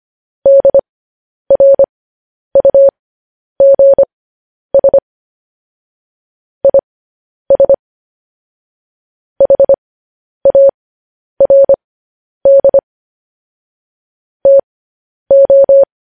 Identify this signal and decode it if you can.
Morse code, heard as a keyed tone, about 25 wpm, spaced as for 10 wpm, 550 Hz. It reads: DRUGS IS HARD TO